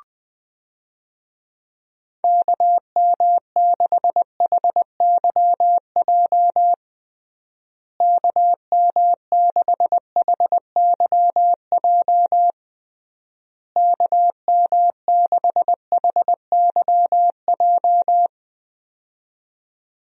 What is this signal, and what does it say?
Morse code, heard as a keyed tone, about 21 wpm, 700 Hz. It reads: KM6HYJ KM6HYJ KM6HYJ
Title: USA Call Signs - 6 Characters - Repeated 3x - 20wpm